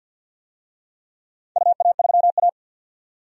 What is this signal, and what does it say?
Morse code, heard as a keyed tone, about 47 wpm, 700 Hz. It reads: UA3A
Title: DXCC-100 Call Signs - 4 Characters - 50wpm